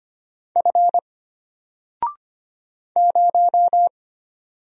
Morse code, heard as a keyed tone, about 25 wpm, 700 Hz. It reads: F 0
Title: Single Character (Letter:Number:Punctuation:Prosign) Alt 25wpm